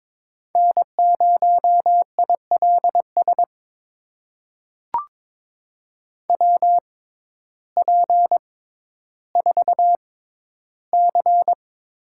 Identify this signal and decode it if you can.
Morse code, heard as a keyed tone, about 22 wpm, spaced as for 12 wpm, 700 Hz. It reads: N0ILS WP4C
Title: USA Call Signs - 6x Character Spacing - 22wpm